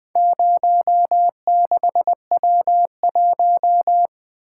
Morse code, heard as a keyed tone, about 20 wpm, 700 Hz. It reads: 06W1
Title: Four Letter-Number Combinations - 20wpm